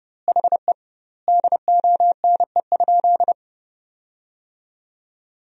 Morse code, heard as a keyed tone, about 32 wpm, 700 Hz. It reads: HE DONE?
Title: 2x Word-Spacing - Sentences from Top 500 Words - 30wpm